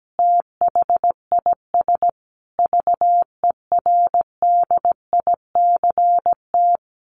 Morse code, heard as a keyed tone, about 17 wpm, 700 Hz. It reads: THIS VERDICT
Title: A Sound Of Thunder 17wpm